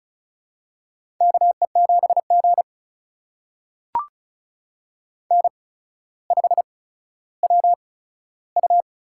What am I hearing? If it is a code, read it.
Morse code, heard as a keyed tone, about 36 wpm, spaced as for 15 wpm, 700 Hz. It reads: KE7G N5WU
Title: USA Call Signs - 8x Character Spacing - 35wpm